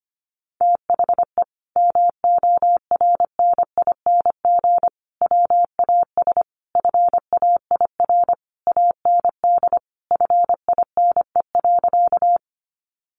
Morse code, heard as a keyed tone, about 25 wpm, 700 Hz. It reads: THE MORNING WAS FAIR AND FINE.
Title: Sentences from Top 1300 Words 25wpm